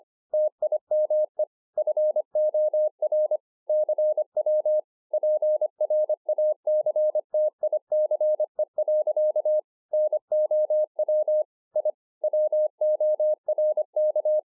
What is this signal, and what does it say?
Morse code, heard as a keyed tone, about 25 wpm, 610 Hz. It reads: TIME FOR CW PRACTICE. NOW I WORK
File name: 2011-04-mp3 / THIS IS NOT MY SHACK 2 25WPM